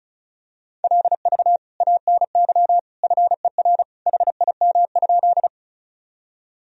Morse code, heard as a keyed tone, about 36 wpm, 700 Hz. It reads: LV ANY FER HIM?
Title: Sentences from Top 400 Words and Abbreviations 35wpm